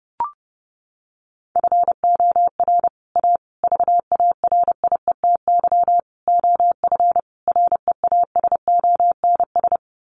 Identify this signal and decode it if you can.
Morse code, heard as a keyed tone, about 30 wpm, 700 Hz. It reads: FOR A VARIETY OF REASONS
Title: Sets of 5 Words Encore 30wpm